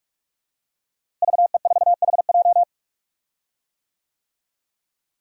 Morse code, heard as a keyed tone, about 46 wpm, 700 Hz. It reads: VE4HJ
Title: DXCC-100 Call Signs - 5 Characters - Repeated 2x - 45wpm